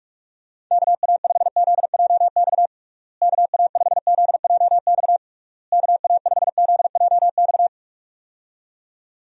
Morse code, heard as a keyed tone, about 46 wpm, 700 Hz. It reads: KA5ZJX KA5ZJX KA5ZJX
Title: USA Call Signs - 6 Characters - Repeated 3x - 45wpm